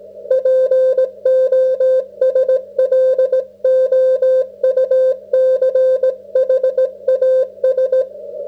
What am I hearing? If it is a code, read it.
Morse code, heard as a keyed tone, about 17 wpm, 530 Hz. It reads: POSLOUCHAS